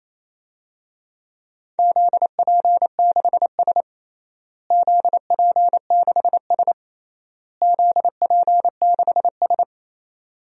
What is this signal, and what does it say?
Morse code, heard as a keyed tone, about 28 wpm, 700 Hz. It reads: ZP6S ZP6S ZP6S
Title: DXCC-100 Call Signs - 4 Characters - Repeated 3x - 28wpm